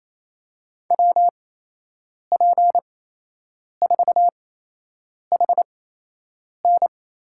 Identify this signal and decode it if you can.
Morse code, heard as a keyed tone, about 28 wpm, spaced as for 8 wpm, 700 Hz. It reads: WP4HN
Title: USA Call Signs - 6 Characters - 8x Character Spacing - 28wpm